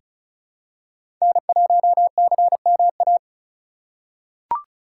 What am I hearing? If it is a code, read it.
Morse code, heard as a keyed tone, about 34 wpm, 700 Hz. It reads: N1CMA
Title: USA Call Signs - 5 Characters - 2x Character Spacing - 35wpm